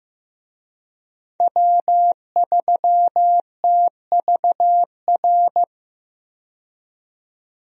Morse code, heard as a keyed tone, about 15 wpm, 700 Hz. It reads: W3TVR
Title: USA Call Signs - 5 Characters - 15wpm